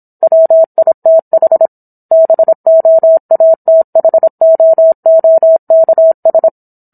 Morse code, heard as a keyed tone, about 27 wpm, 650 Hz. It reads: WITH BOATHOOKS